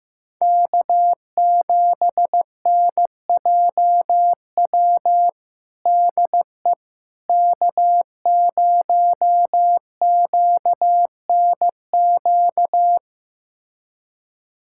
Morse code, heard as a keyed tone, about 15 wpm, 700 Hz. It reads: K7NJW DE K0QNQ
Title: USA Callsign DE Callsign - 5 Characters - 15wpm